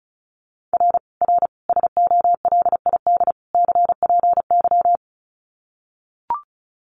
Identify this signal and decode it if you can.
Morse code, heard as a keyed tone, about 34 wpm, 700 Hz. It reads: R R SOLID CPY
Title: Common QSO Phrases 35wpm